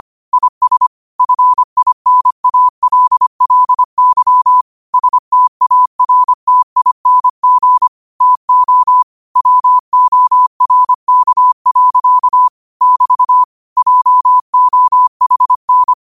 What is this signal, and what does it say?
Morse code, heard as a keyed tone, about 25 wpm, 1000 Hz. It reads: IS FINALLY STARTING TO WORK. <BT> JOHN
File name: qotdfiles / 25wpm